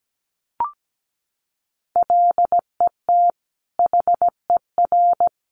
Morse code, heard as a keyed tone, about 17 wpm, 700 Hz. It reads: LET HER
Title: Sentences from Top 500 Words 17wpm